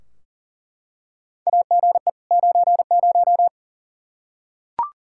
Morse code, heard as a keyed tone, about 40 wpm, 700 Hz. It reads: AGE 90
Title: Practice QSO - Exchange 3 - More Info - 40wpm